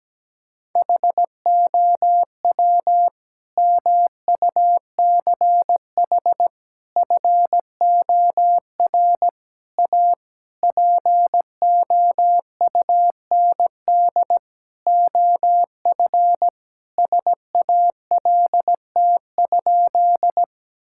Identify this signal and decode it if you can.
Morse code, heard as a keyed tone, about 17 wpm, 700 Hz. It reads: HOW MUCH FOR A POUND OF SALT?
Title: Sentences from Top 1000 Words - Review Part 5 - 17wpm